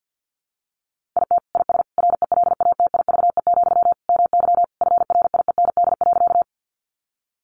Morse code, heard as a keyed tone, about 50 wpm, 700 Hz. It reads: IT IS RELATIVE, MY FRIEND.
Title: Sentences from Top 1400 Words 50wpm